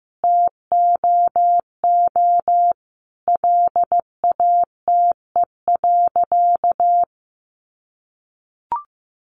Morse code, heard as a keyed tone, about 15 wpm, 700 Hz. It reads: TOO LATE.
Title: Sentences from Top 300 Words 15wpm